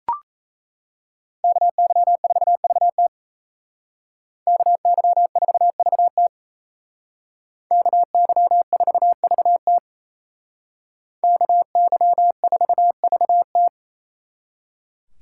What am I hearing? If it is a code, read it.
Morse code, heard as a keyed tone, about 32 wpm, 700 Hz. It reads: KY4VT KY4VT KY4VT KY4VT
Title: Speed Racing - USA 2x2 Call Signs 28wpm